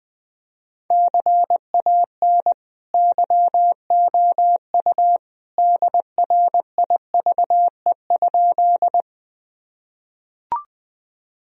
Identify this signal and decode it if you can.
Morse code, heard as a keyed tone, about 20 wpm, 700 Hz. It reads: CAN YOU DRIVE?
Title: Sentences from Top 1000 Words - Review Part 2 - 20wpm